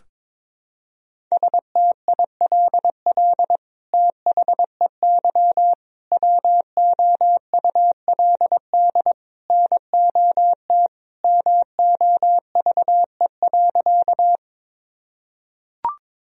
Morse code, heard as a keyed tone, about 22 wpm, 700 Hz. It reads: STILL THEY WOULD NOT MOVE.
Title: Sentences from Top 1000 Words - Review Part 1 - 22wpm